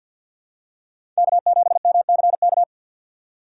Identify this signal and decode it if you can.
Morse code, heard as a keyed tone, about 53 wpm, 700 Hz. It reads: K7MCK
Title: USA Call Signs - 5 Characters - 2x Character Spacing - 50wpm